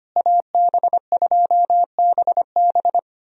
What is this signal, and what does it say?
Morse code, heard as a keyed tone, about 25 wpm, 700 Hz. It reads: AB2BB
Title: USA Call Signs - 5 Characters - Repeated 3x - 25wpm